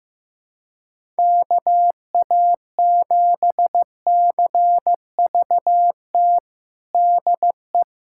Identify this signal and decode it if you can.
Morse code, heard as a keyed tone, about 15 wpm, 700 Hz. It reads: KA7CVT DE
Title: USA Callsign DE Callsign 15wpm